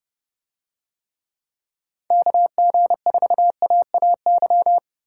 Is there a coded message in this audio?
Yes: Morse code from 2.1 to 4.8 s, heard as a keyed tone, about 30 wpm, 700 Hz. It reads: KG4AAY